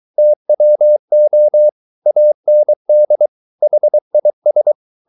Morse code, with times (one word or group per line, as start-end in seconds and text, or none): 0.2-1.7 TWO
2.1-3.3 AND
3.6-4.7 HIS